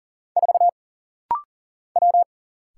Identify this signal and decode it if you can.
Morse code, heard as a keyed tone, about 40 wpm, 700 Hz. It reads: 4 W
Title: Single Letter-Number Alt - Rapid-Fire 40wpm